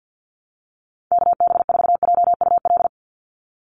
Morse code, heard as a keyed tone, about 50 wpm, 700 Hz. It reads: KB4JUL
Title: USA Call Signs 50wpm